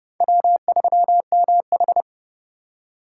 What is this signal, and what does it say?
Morse code, heard as a keyed tone, about 30 wpm, 700 Hz. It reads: W3MH
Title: USA Call Signs - 4 Characters - 2x Character Spacing - 30wpm